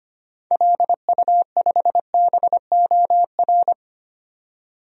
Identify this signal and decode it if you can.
Morse code, heard as a keyed tone, about 25 wpm, 700 Hz. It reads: LU5BOR
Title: DXCC-100 Call Signs - 6 Characters - 25wpm